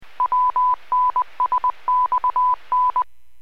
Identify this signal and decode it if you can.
Morse code, heard as a keyed tone, about 20 wpm, 1000 Hz. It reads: WNSXN